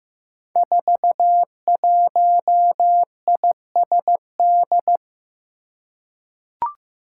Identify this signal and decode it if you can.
Morse code, heard as a keyed tone, about 16 wpm, 700 Hz. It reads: 41ISD
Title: Five Challenging Letter-Number Combinations 15wpm